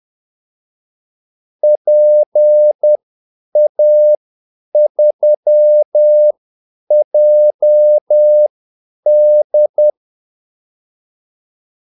Morse code, heard as a keyed tone, about 10 wpm, spaced as for 8 wpm, 600 Hz. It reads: PA3JD
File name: Callsigns-10-8wpm